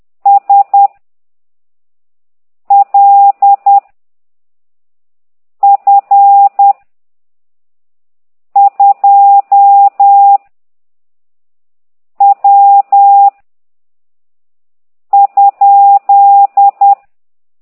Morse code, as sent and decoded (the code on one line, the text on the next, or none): ... / .-.. / ..-. / ..--- / .-- / ..--..
S L F 2 W ?